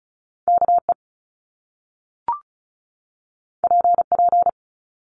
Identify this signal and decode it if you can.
Morse code, heard as a keyed tone, about 36 wpm, 700 Hz. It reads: KE PP
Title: Top Two Letter Combinations in Words 35wpm